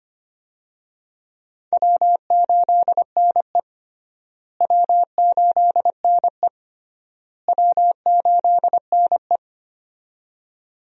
Morse code, heard as a keyed tone, about 25 wpm, 700 Hz. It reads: W8NE W8NE W8NE
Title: USA Call Signs - 4 Characters - Repeated 3x - 25wpm